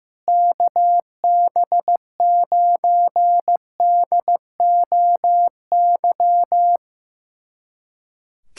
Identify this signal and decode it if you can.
Morse code, heard as a keyed tone, about 16 wpm, 700 Hz. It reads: KB9DOY
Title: USA Callsign DE Callsign - 6 Characters - 15wpm